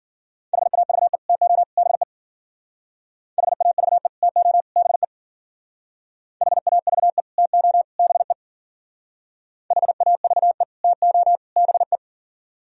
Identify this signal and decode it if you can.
Morse code, heard as a keyed tone, about 49 wpm, 700 Hz. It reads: HAVE TO BE HAVE TO BE HAVE TO BE HAVE TO BE
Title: Speed-Racing - Sets of 3 Words 40wpm